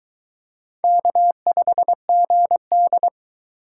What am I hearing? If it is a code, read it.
Morse code, heard as a keyed tone, about 22 wpm, 700 Hz. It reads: K5GD